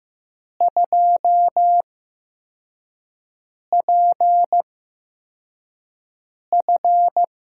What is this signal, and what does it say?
Morse code, heard as a keyed tone, about 15 wpm, spaced as for 4 wpm, 700 Hz. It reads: 2PF
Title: USA Call Signs - 4 Characters - 8x Character Spacing - 15wpm